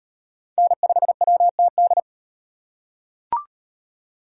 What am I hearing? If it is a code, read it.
Morse code, heard as a keyed tone, about 38 wpm, 700 Hz. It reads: N5WTD